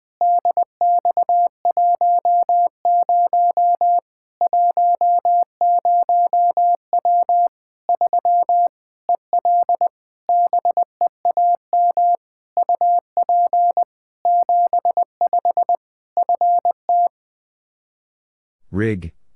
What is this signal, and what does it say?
Morse code, heard as a keyed tone, about 20 wpm, 700 Hz. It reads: DX10 10W 3 EL BEAM UP 75 FT
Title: K1USN SST QSO Practice 20wpm